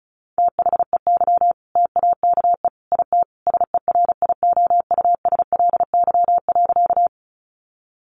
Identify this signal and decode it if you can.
Morse code, heard as a keyed tone, about 34 wpm, 700 Hz. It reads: THEY TAKE IT SERIOUSLY.
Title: Sentences from Top 1200 Words 35wpm